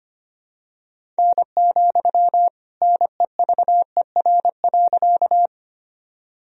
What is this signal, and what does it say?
Morse code, heard as a keyed tone, about 25 wpm, 700 Hz. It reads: N, NEVER.
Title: Sentences from Top 300 Words and Abbreviations 25wpm